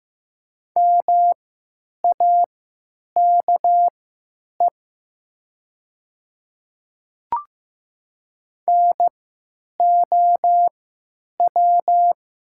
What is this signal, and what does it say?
Morse code, heard as a keyed tone, about 15 wpm, spaced as for 9 wpm, 700 Hz. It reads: MAKE NOW
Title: 3x Character Spacing - Top 300 Words - 15wpm